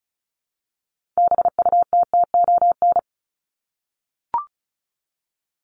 Morse code, heard as a keyed tone, about 34 wpm, 700 Hz. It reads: BUTTON
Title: Top 1001-1250 Words 35wpm